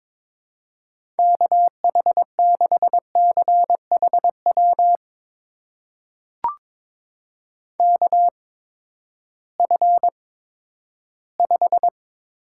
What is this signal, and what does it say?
Morse code, heard as a keyed tone, about 22 wpm, spaced as for 13 wpm, 700 Hz. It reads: KH6CHW KF5
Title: USA Call Signs - 6 Characters - 8x Character Spacing - 22wpm